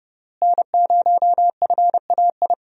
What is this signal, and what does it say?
Morse code, heard as a keyed tone, about 30 wpm, 700 Hz. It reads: N0FAI
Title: USA Call Signs - 5 Characters - Repeated 3x - 30wpm